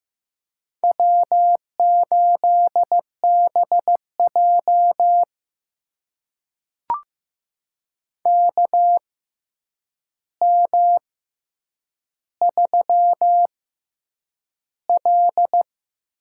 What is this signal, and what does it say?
Morse code, heard as a keyed tone, about 15 wpm, spaced as for 8 wpm, 700 Hz. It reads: W8BJ KM3L
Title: USA Call Signs - 4 Characters - 6x Character Spacing - 15wpm